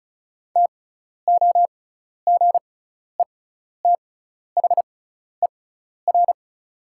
Morse code, heard as a keyed tone, about 36 wpm, spaced as for 12 wpm, 700 Hz. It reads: TOGETHER
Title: 6x Character Spacing - Top 300 Words - 35wpm